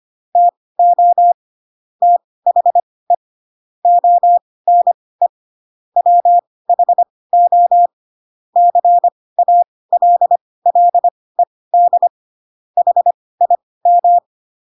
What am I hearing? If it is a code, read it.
Morse code, heard as a keyed tone, about 25 wpm, spaced as for 18 wpm, 700 Hz. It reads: TO THE ONE WHO CALLED HIM
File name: Pinocchio / Pinocchio ch2 12 25 ik1hge